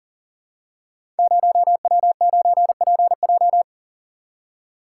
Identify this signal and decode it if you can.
Morse code, heard as a keyed tone, about 38 wpm, 700 Hz. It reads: 0W9PJ
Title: Code Groups - 5 Letter:Numbers - 40wpm